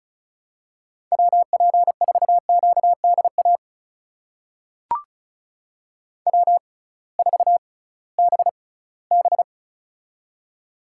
Morse code, heard as a keyed tone, about 36 wpm, spaced as for 20 wpm, 700 Hz. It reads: WP4QDA W4BB
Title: USA Call Signs - 6x Character Spacing - 35wpm